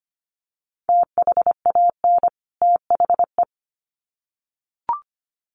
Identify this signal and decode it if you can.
Morse code, heard as a keyed tone, about 25 wpm, 700 Hz. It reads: THAN THE
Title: Sets of 2 Words 25wpm